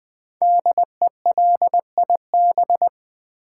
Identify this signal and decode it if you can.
Morse code, heard as a keyed tone, about 20 wpm, 700 Hz. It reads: DELIB
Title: Build A Word - 5 Letters At A Time - 12-Letter Words - 20wpm